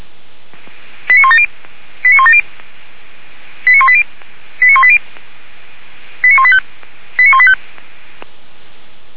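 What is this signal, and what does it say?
Five-tone selective call (ZVEI), heard as quick runs of five tones, tones of about 80 ms. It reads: 89170 89170 89180 89180 89185 89185
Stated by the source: ZVEI five-tone selective call 89110 received by SDR.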